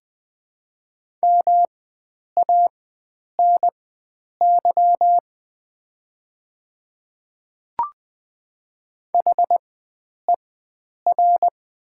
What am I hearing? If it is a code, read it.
Morse code, heard as a keyed tone, about 21 wpm, spaced as for 9 wpm, 700 Hz. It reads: MANY HER
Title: 4x Character Spacing - Top 100 Words - 20wpm